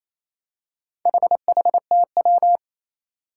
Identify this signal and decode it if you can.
Morse code, heard as a keyed tone, about 27 wpm, 700 Hz. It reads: HHTW
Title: Four Challenging Letter-Number Combinations 28wpm